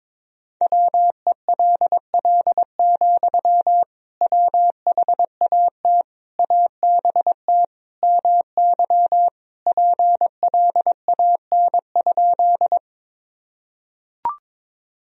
Morse code, heard as a keyed tone, about 22 wpm, 700 Hz. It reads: WELL, WHAT ABT MY PLAN?
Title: Sentences from Top 500 Words and Abbreviations 22wpm